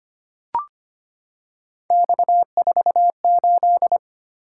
Morse code, heard as a keyed tone, about 25 wpm, 700 Hz. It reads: X48
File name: Three Letter-Number Combinations - 25wpm